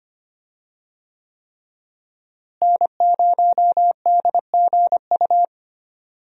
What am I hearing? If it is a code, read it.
Morse code, heard as a keyed tone, about 25 wpm, 700 Hz. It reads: N0DGU
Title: USA Call Signs - 5 Characters - Repeated 3x - 25wpm